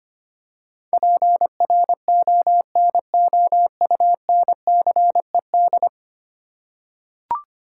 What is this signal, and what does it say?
Morse code, heard as a keyed tone, about 25 wpm, 700 Hz. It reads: PRONOUNCED
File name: Ten Letter Words - 25wpm